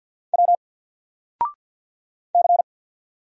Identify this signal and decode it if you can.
Morse code, heard as a keyed tone, about 48 wpm, 700 Hz. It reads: W C
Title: Single Character (Letter:Number:Punctuation:Prosign) Alt 50wpm